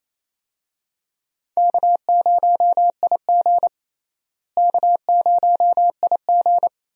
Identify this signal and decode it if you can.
Morse code, heard as a keyed tone, about 28 wpm, 700 Hz. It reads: K0IG K0IG
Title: USA Call Signs - 4 Characters - Repeated 2x - 28wpm